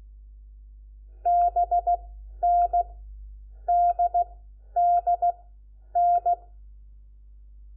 Morse code, heard as a keyed tone, about 16 wpm, 700 Hz. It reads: B N D D N
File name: Lecciones / CLASE Nº4